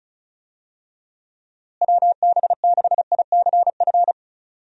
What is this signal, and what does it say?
Morse code, heard as a keyed tone, about 36 wpm, 700 Hz. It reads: WB6ICF
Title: USA Call Signs - 6 Characters - Repeated 2x - 35wpm